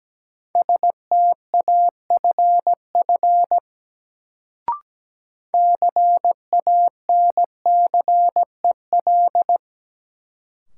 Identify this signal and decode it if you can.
Morse code, heard as a keyed tone, about 17 wpm, 700 Hz. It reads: STAFF CANCEL
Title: Sentences from Top 1200 Words - New Word Drill - 17wpm